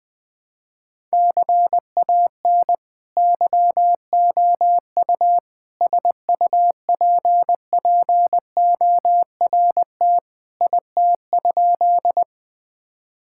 Morse code, heard as a keyed tone, about 21 wpm, 700 Hz. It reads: CAN YOU SUPPORT IT?
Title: Sentences from Top 1000 Words - Review Part 5 - 20wpm